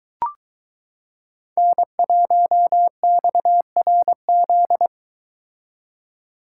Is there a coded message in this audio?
Yes: Morse code from 1.6 to 4.9 s, heard as a keyed tone, about 22 wpm, 700 Hz. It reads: N1XRZ